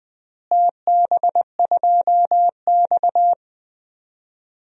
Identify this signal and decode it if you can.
Morse code, heard as a keyed tone, about 21 wpm, 700 Hz. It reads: TB2X
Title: DXCC-100 Call Signs - 4 Characters - Repeated 2x - 20wpm